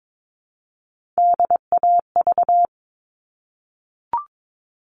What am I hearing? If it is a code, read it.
Morse code, heard as a keyed tone, about 22 wpm, 700 Hz. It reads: DAV
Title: All Cut Numbers - 3-Digit 22wpm